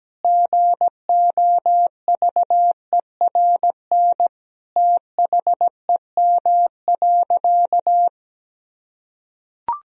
Morse code, heard as a keyed tone, about 17 wpm, 700 Hz. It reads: GOVERN THEM.
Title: Sentences from Top 1000 Words - Review Part 2 - 17wpm